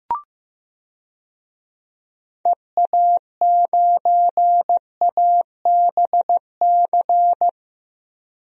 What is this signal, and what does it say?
Morse code, heard as a keyed tone, about 15 wpm, 700 Hz. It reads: EA9ABC
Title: DXCC-100 Call Signs - 6 Characters - 15wpm